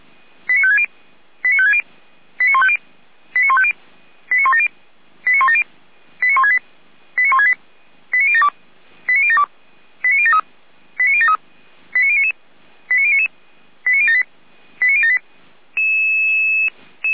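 Five-tone selective call (ZVEI), heard as quick runs of five tones, tones of about 70 ms. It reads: 89560 89560 89150 89150 89170 89170 89168 89168 89072 89072 89073 89073 89099 89099 89078 89078
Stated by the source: ZVEI five-tone selective call received with an SDR, from feuerwehr-orlamuende (Orlamünde fire brigade).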